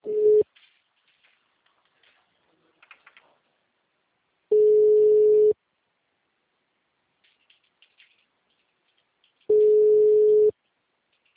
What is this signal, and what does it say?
Ringback tone (Europe, 425 Hz).
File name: out- -2007- - - .20